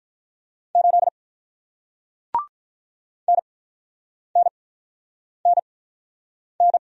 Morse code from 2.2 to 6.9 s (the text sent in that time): NNNN